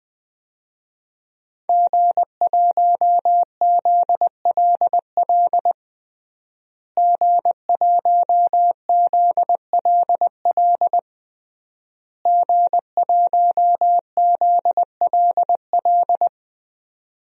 Morse code, heard as a keyed tone, about 21 wpm, 700 Hz. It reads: G1ZLL G1ZLL G1ZLL
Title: DXCC-100 Call Signs - 5 Characters - Repeated 3x - 20wpm